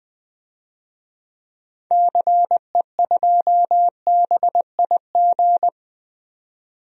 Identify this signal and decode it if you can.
Morse code, heard as a keyed tone, about 20 wpm, 700 Hz. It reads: CE2BIG
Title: DXCC-100 Call Signs - 6 Characters - 20wpm